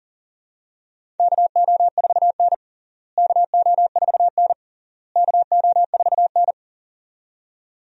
Morse code, heard as a keyed tone, about 40 wpm, 700 Hz. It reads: KO4N KO4N KO4N
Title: USA Call Signs - 4 Characters - Repeated 3x - 40wpm